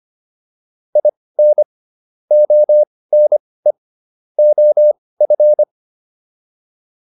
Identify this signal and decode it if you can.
Morse code, heard as a keyed tone, about 25 wpm, spaced as for 18 wpm, 600 Hz. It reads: IN ONE OF